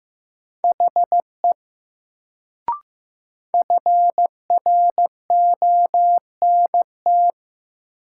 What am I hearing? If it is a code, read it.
Morse code, heard as a keyed tone, about 15 wpm, 700 Hz. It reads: HE FRONT
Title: Sets of 2 Words - New Word Drill - 15wpm